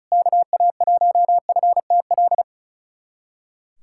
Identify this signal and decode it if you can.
Morse code, heard as a keyed tone, about 35 wpm, 700 Hz. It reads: KA1FTL
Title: USA Callsign DE Callsign - 6 Characters - 35wpm